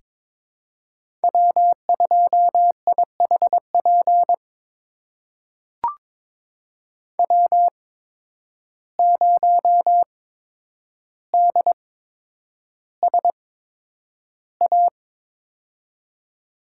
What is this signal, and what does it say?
Morse code, heard as a keyed tone, about 22 wpm, spaced as for 9 wpm, 700 Hz. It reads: W2IHP W0DSA
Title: USA Call Signs - 5 Characters - 8x Character Spacing - 22wpm